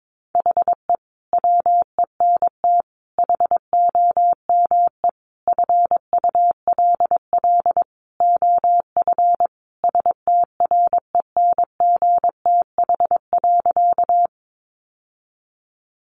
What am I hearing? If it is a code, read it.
Morse code, heard as a keyed tone, about 22 wpm, 700 Hz. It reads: HE WENT HOME FULL OF STRENGTH.
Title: Sentences from Top 1300 Words 22wpm